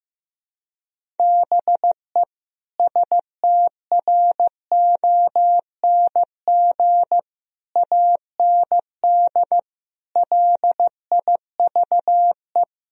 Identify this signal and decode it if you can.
Morse code, heard as a keyed tone, about 14 wpm, 700 Hz. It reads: BE STRONG AND LIVE
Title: Sentences from Top 1000 Words - Review Part 3 - 15wpm